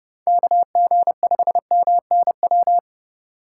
Morse code, heard as a keyed tone, about 32 wpm, 700 Hz. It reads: KG5MNW
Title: USA Call Signs - 6 Characters - 8x Character Spacing - 30wpm